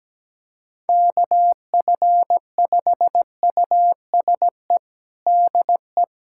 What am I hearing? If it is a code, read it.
Morse code, heard as a keyed tone, about 17 wpm, 700 Hz. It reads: KF5USE DE
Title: USA Callsign DE Callsign 17wpm